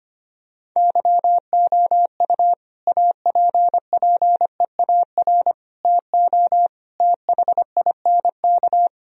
Morse code, heard as a keyed tone, about 25 wpm, 700 Hz. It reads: YOU APPEAR TO THINK
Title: Sentences from Top 1000 Words - Review Part 2 - 25wpm